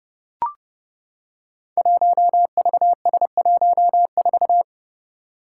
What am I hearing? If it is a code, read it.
Morse code, heard as a keyed tone, about 30 wpm, 700 Hz. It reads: 1VS14
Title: Five Challenging Letter-Number Combinations 30wpm